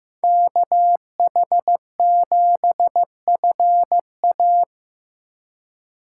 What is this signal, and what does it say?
Morse code, heard as a keyed tone, about 15 wpm, 700 Hz. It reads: KH7FA
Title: USA Call Signs - 5 Characters - 4x Character Spacing - 15wpm